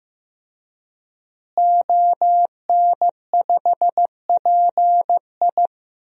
Morse code, heard as a keyed tone, about 15 wpm, 700 Hz. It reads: ON5PI
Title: DXCC-100 Call Signs - 5 Characters - Repeated 3x - 15wpm